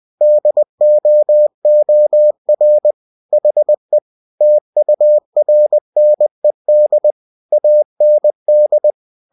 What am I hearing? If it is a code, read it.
Morse code, heard as a keyed tone, about 20 wpm, 600 Hz. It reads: DOOR HE TURNED AND